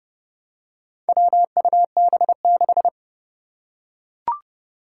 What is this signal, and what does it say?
Morse code, heard as a keyed tone, about 30 wpm, 700 Hz. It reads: WUB6
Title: Four Challenging Letter-Number Combinations 30wpm